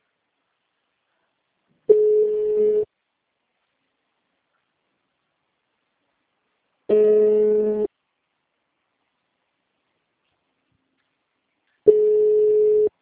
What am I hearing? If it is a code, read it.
Ringback tone (Europe, 425 Hz).
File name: out- -2015- - - .60545